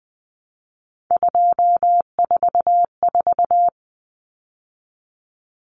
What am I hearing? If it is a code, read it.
Morse code, heard as a keyed tone, about 20 wpm, 700 Hz. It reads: 244
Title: Numbers - 3-Digit 20wpm